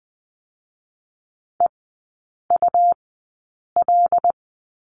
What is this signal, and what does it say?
Morse code, heard as a keyed tone, about 20 wpm, spaced as for 8 wpm, 700 Hz. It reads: EUL
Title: 107 Lesson 18e - Letter m - Three Characters 20wpm